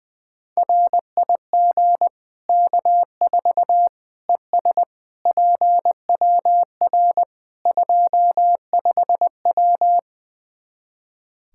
Morse code, heard as a keyed tone, about 20 wpm, 700 Hz. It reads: RIG K4 ES PWR 25W
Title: Practice QSO - Exchange 2 - Info - 20wpm